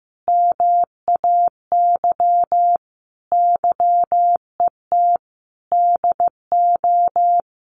Morse code, heard as a keyed tone, about 15 wpm, 700 Hz. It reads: MAY YET DO
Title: Sentences from Top 1400 Words 15wpm